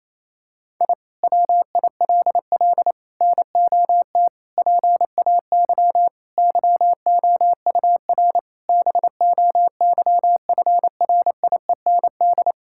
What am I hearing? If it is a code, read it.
Morse code, heard as a keyed tone, about 28 wpm, 700 Hz. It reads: I WILL NOT PAY YOUR BOYFRIEND
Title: Sentences from Top 1000 Words - Review Part 5 - 28wpm